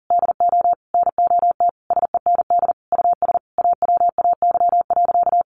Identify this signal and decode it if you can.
Morse code, heard as a keyed tone, about 40 wpm, 700 Hz. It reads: DO NOT SEND US AWAY.
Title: Sentences from Top 1500 Words 40wpm